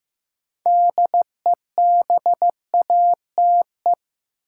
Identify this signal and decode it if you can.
Morse code, heard as a keyed tone, about 15 wpm, 700 Hz. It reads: DEBATE
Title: Sentences from Top 1100 Words - New Word Drill - 15wpm